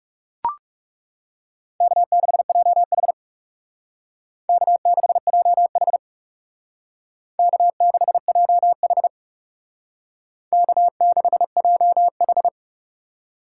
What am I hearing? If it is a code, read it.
Morse code, heard as a keyed tone, about 36 wpm, 700 Hz. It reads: K6JH K6JH K6JH K6JH
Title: Speed Racing - USA 1x2 Call Signs 30wpm